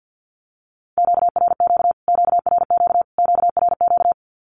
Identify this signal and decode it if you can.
Morse code, heard as a keyed tone, about 50 wpm, 700 Hz. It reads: QRQ QRQ QRQ
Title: Q-Codes Repeated 3x 50wpm